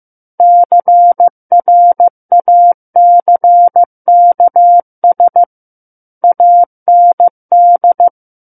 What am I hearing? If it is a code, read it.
Morse code, heard as a keyed tone, about 15 wpm, 700 Hz. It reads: CRACKS AND
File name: Chapter 6 - 15 WPM